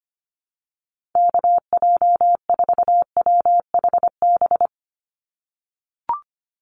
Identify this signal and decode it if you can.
Morse code, heard as a keyed tone, about 25 wpm, 690 Hz. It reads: KJ4WHB
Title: USA Call Signs 25wpm